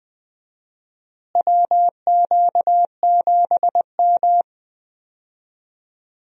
Morse code, heard as a keyed tone, about 20 wpm, 700 Hz. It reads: WQ7M
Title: USA Call Signs - 4 Characters - Repeated 3x - 20wpm